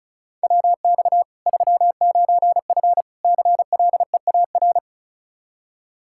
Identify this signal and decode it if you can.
Morse code, heard as a keyed tone, about 36 wpm, 700 Hz. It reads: WX 39F CLEAR
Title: Practice QSO - Exchange 2 - Info - 35wpm